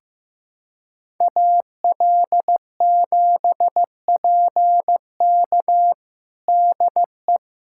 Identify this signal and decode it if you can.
Morse code, heard as a keyed tone, about 15 wpm, 700 Hz. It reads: AL7PK DE
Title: USA Callsign DE Callsign - 5 Characters - 15wpm